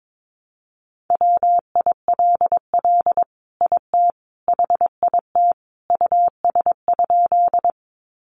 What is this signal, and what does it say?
Morse code, heard as a keyed tone, about 22 wpm, 700 Hz. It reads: WILL IT HIT US?
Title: Sentences from Top 1200 Words 22wpm